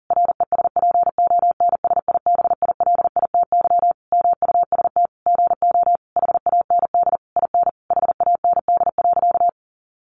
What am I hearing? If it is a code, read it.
Morse code, heard as a keyed tone, about 40 wpm, 700 Hz. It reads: RESPONSIBILITY MUST GO HAND IN HAND.
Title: Sentences from Top 1200 Words 40wpm